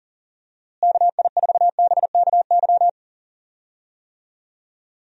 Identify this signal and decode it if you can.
Morse code, heard as a keyed tone, about 37 wpm, 700 Hz. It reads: KI4BKY
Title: USA Call Signs - 6 Characters - Repeated 2x - 40wpm